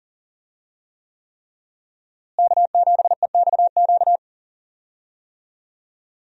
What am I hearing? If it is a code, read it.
Morse code, heard as a keyed tone, about 40 wpm, 700 Hz. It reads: K7EXQ
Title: USA Call Signs - 5 Characters - Repeated 3x - 40wpm